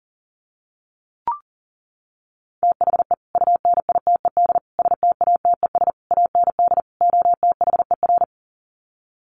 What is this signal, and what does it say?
Morse code, heard as a keyed tone, about 40 wpm, 700 Hz. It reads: THE UNITED STATES AND OTHER
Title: Sets of 5 Words Encore 40wpm